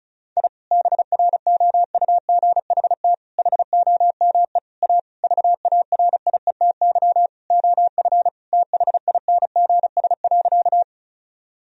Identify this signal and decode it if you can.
Morse code, heard as a keyed tone, about 36 wpm, 700 Hz. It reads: I BROUGHT HOME A VARIETY OF THINGS.
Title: Sentences from Top 1000 Words - Review Part 4 - 35wpm